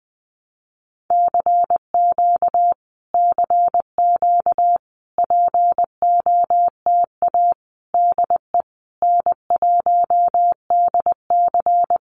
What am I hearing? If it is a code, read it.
Morse code, heard as a keyed tone, about 20 wpm, 700 Hz. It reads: CQ CQ POTA DE N1DC
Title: Common QSO Phrases Encore 20wpm